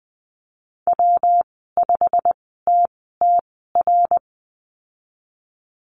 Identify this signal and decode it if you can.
Morse code, heard as a keyed tone, about 20 wpm, spaced as for 15 wpm, 700 Hz. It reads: W5TTR
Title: 179 Lesson 29g - Slash - Call Signs 20wpm